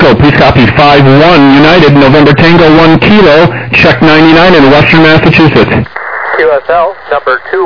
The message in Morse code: none